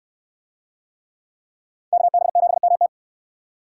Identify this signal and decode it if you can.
Morse code, heard as a keyed tone, about 62 wpm, 700 Hz. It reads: KB7MT